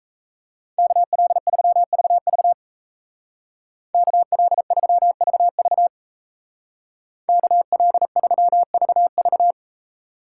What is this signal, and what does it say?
Morse code, heard as a keyed tone, about 35 wpm, 700 Hz. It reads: KL3VV KL3VV KL3VV
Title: Speed Racing - USA 2x2 Call Signs 28wpm